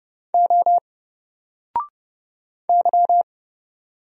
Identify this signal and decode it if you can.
Morse code, heard as a keyed tone, about 30 wpm, 700 Hz. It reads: O Y
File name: Single-Letter-Number-30wpm